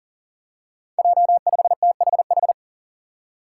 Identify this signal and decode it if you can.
Morse code, heard as a keyed tone, about 40 wpm, 700 Hz. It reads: J5THH
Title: Five Challenging Letter-Number Combinations 40wpm